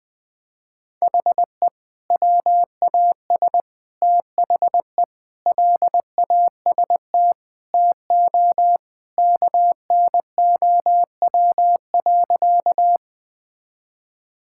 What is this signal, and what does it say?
Morse code, heard as a keyed tone, about 20 wpm, 700 Hz. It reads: HE WAS THE LAST TO KNOW.
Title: Sentences with 2-Word Phrases - 20wpm